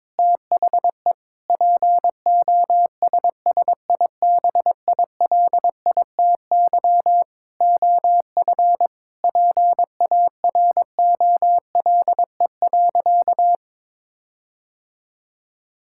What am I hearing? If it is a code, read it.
Morse code, heard as a keyed tone, about 22 wpm, 700 Hz. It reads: THE POSSIBILITY OF PAROLE.
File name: practice / 22wpm / Sentences with 8-Word Phrases - 22wpm